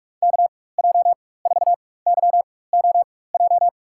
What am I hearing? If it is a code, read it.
Morse code, heard as a keyed tone, about 42 wpm, spaced as for 21 wpm, 700 Hz. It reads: KJ4YOJ
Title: USA Call Signs - 6 Characters - 4x Character Spacing - 45wpm